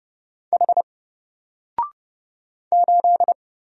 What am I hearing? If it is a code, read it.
Morse code, heard as a keyed tone, about 31 wpm, 700 Hz. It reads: H 8
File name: Single-Letter-Number-30wpm